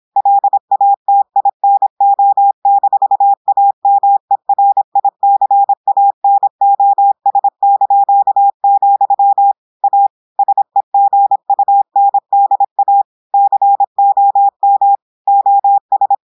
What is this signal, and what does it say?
Morse code, heard as a keyed tone, about 26 wpm, 800 Hz. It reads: LATINO-AMERICANOS), A SEGUNDA COM OS
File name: audios / portugais / portugais texte en clair 1 vitesse 26 mots minutes